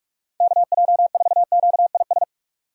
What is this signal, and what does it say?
Morse code, heard as a keyed tone, about 46 wpm, 700 Hz. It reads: KJ4QIS
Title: USA Call Signs - 2x Character Spacing - 45wpm